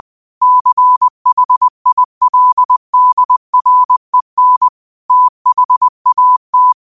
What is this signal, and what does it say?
Morse code, heard as a keyed tone, about 20 wpm, 1000 Hz. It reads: CHILDREN THAT